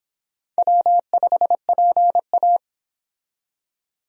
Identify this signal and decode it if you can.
Morse code, heard as a keyed tone, about 25 wpm, 700 Hz. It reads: W5PA